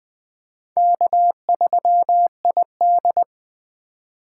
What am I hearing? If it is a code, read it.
Morse code, heard as a keyed tone, about 21 wpm, 700 Hz. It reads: K3ID
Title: Speed Racing - USA 1x2 Call Signs 17wpm